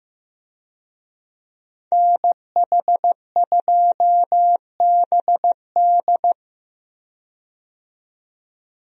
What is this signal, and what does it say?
Morse code, heard as a keyed tone, about 15 wpm, 700 Hz. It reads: NH2BD
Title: USA Call Signs - 5 Characters - 15wpm